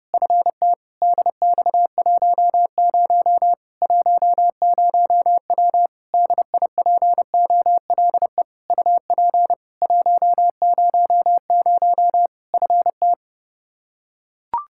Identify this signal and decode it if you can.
Morse code, heard as a keyed tone, about 30 wpm, 700 Hz. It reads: FT DX10 10W DIPOLE UP 100 FT
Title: Practice QSO - Exchange 2 - Info - 30wpm